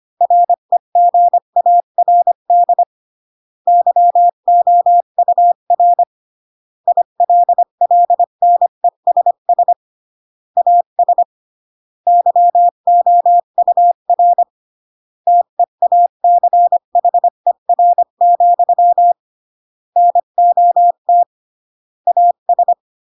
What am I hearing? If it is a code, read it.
Morse code, heard as a keyed tone, about 25 wpm, 700 Hz. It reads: REGARD YOUR ILLNESS AS YOUR TEACHER, NOT AS